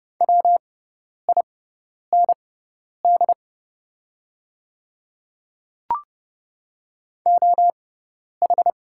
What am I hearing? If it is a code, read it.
Morse code, heard as a keyed tone, about 30 wpm, spaced as for 10 wpm, 690 Hz. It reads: WIND OH
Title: 6x Character Spacing - Top 500 Words - 30wpm